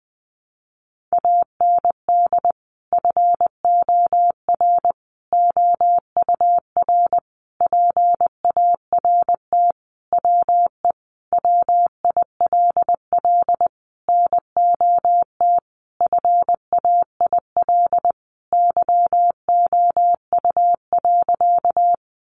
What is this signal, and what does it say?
Morse code, heard as a keyed tone, about 20 wpm, 700 Hz. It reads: AND FOR OUR PART WE WILL NOT FAIL YOU.
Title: Sentences from Top 1100 Words 20wpm